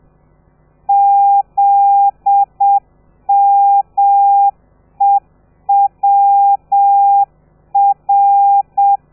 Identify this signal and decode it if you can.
Morse code, heard as a keyed tone, about 7 wpm, 800 Hz. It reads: ZMEWR
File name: telegrafia / i2tfj / Lez13 30